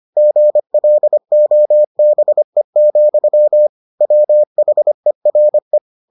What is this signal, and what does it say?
Morse code, heard as a keyed tone, about 25 wpm, 600 Hz. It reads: GLOBE, WHERE